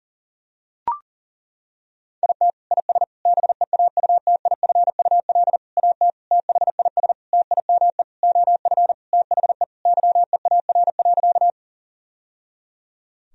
Morse code, heard as a keyed tone, about 40 wpm, 700 Hz. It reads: IT IS BEAUTIFUL AT THIS TIME OF THE YEAR.
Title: Sentences from Top 1000 Words - Review Part 1 - 40wpm